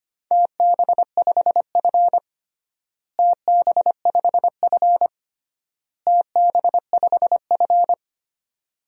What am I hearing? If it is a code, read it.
Morse code, heard as a keyed tone, about 25 wpm, 700 Hz. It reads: TB5F TB5F TB5F
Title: DXCC-100 Call Signs - 4 Characters - Repeated 3x - 25wpm